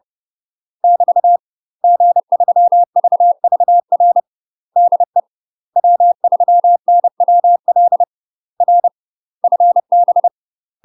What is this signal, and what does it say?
Morse code, heard as a keyed tone, about 30 wpm, 700 Hz. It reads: <BT> G3VVR DE W3NWL R FB
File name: UK MASTER QSOs TRAINING / 30WPM / UK MASTER QSOs 0043